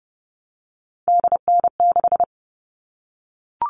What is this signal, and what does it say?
Morse code, heard as a keyed tone, about 30 wpm, 700 Hz. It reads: DN6
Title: All Cut Numbers - 3-Digit 30wpm